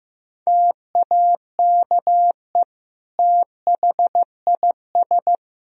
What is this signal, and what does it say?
Morse code, heard as a keyed tone, about 15 wpm, 700 Hz. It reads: TAKE THIS
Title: Sentences from Top 1000 Words - Review Part 4 - 15wpm